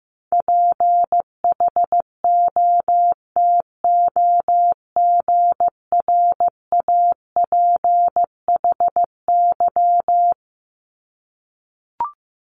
Morse code, heard as a keyed tone, about 16 wpm, 700 Hz. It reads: PHOTOGRAPHY
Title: Multi-letter Phonemes Repeated 3x 15wpm